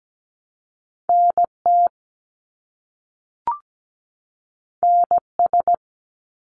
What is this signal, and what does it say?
Morse code, heard as a keyed tone, about 17 wpm, 700 Hz. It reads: NT NS
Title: Canadian Provinces and Territories 17wpm